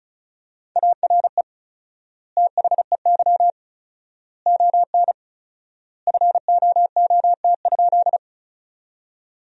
Morse code, heard as a keyed tone, about 36 wpm, 700 Hz. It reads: ARE THEY ON FOOT?
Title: 4x Word-Spacing - Sentences from Top 500 Words - 35wpm